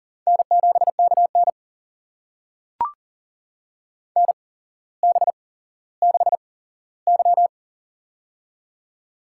Morse code, heard as a keyed tone, about 39 wpm, spaced as for 17 wpm, 700 Hz. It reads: N7KN NB6Y
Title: USA Call Signs - 4 Characters - 8x Character Spacing - 40wpm